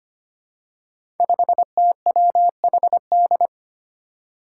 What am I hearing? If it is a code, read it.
Morse code, heard as a keyed tone, about 25 wpm, 700 Hz. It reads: 5TWHD
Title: Five Challenging Letter-Number Combinations 25wpm